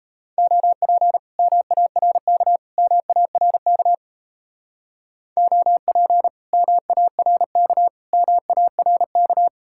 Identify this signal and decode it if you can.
Morse code, heard as a keyed tone, about 37 wpm, 700 Hz. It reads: OP MARK MARK OP MARK MARK